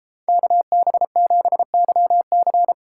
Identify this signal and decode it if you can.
Morse code, heard as a keyed tone, about 33 wpm, 700 Hz. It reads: KB7YC